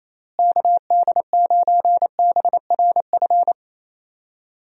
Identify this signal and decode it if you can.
Morse code, heard as a keyed tone, about 28 wpm, 700 Hz. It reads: KD9BRF
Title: USA Call Signs - 6 Characters - Repeated 2x - 28wpm